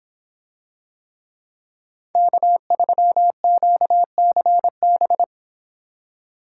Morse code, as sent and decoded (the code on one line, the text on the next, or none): -.- ...-- --.- -.-. -...
K3QCB